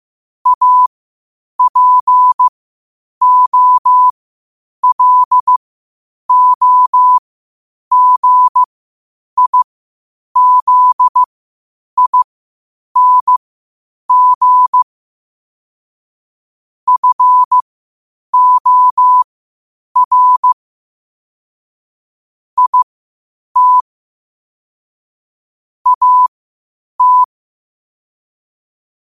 Morse code, heard as a keyed tone, about 15 wpm, spaced as for 8 wpm, 1000 Hz. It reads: APOLOGIZING FOR IT AT
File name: qotdfiles / 8wpm